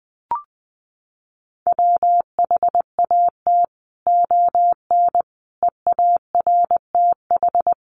Morse code, heard as a keyed tone, about 19 wpm, 700 Hz. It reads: WHAT ON EARTH
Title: Sentences from Top 1500 Words 20wpm